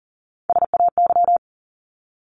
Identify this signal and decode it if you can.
Morse code, heard as a keyed tone, about 40 wpm, 700 Hz. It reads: SAY
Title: ES:S Suffix - Top 100 Words with Base Words 40wpm